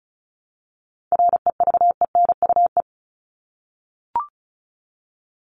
Morse code, heard as a keyed tone, about 34 wpm, 700 Hz. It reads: REVENUE
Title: Top 1501-1750 Words 35wpm